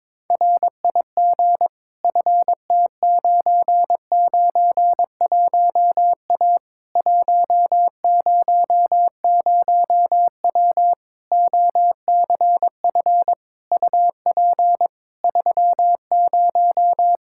Morse code, heard as a keyed tone, about 22 wpm, 700 Hz. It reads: RIG FT991A 100W OCF UP 30
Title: Practice QSO - Exchange 2 - Info - 22wpm